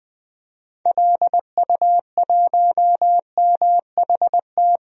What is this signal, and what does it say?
Morse code, heard as a keyed tone, about 20 wpm, 700 Hz. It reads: LU1MHT
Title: DXCC-100 Call Signs - 6 Characters - 20wpm